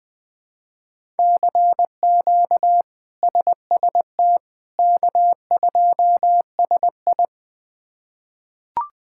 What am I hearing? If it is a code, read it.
Morse code, heard as a keyed tone, about 20 wpm, 700 Hz. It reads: CQ SST K2SI
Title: K1USN SST QSO Practice - 20wpm